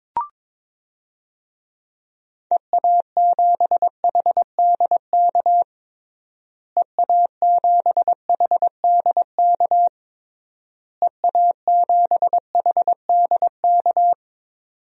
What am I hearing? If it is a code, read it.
Morse code, heard as a keyed tone, about 22 wpm, 700 Hz. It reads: EA7HDK EA7HDK EA7HDK
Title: DXCC-100 Call Signs - 6 Characters - Repeated 3x - 22wpm